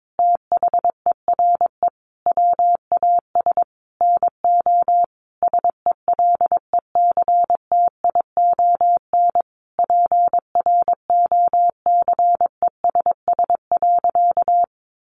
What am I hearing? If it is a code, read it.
Morse code, heard as a keyed tone, about 22 wpm, 700 Hz. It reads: THERE WAS NO SELECTION PROCESS.
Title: Sentences from Top 1400 Words 22wpm